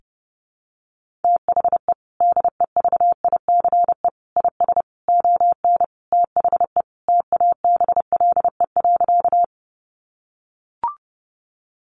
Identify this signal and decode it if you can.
Morse code, heard as a keyed tone, about 30 wpm, 700 Hz. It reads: THE DEVICE IS ON THE TABLE.
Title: Sentences from Top 1500 Words 30wpm